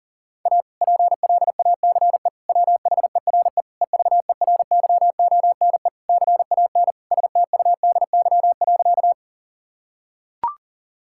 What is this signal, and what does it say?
Morse code, heard as a keyed tone, about 40 wpm, 700 Hz. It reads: A PLACE WHERE EVERYONE CAN STUDY.
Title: Sentences from Top 1000 Words - Review Part 5 - 40wpm